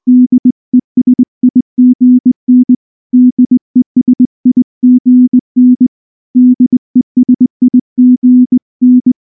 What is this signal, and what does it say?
Morse code, heard as a keyed tone, about 20 wpm, 250 Hz. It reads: DESIGN DESIGN DESIGN
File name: morse / morse-training-20WPM-4